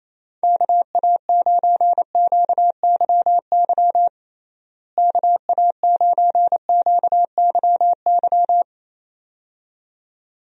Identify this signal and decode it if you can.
Morse code, heard as a keyed tone, about 28 wpm, 700 Hz. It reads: KA9QYY KA9QYY
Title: USA Call Signs - 6 Characters - Repeated 2x - 28wpm